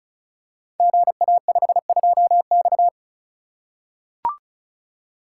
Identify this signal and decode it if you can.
Morse code, heard as a keyed tone, about 36 wpm, 700 Hz. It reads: GA52X
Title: Code Groups - 5 Letter:Numbers - 35wpm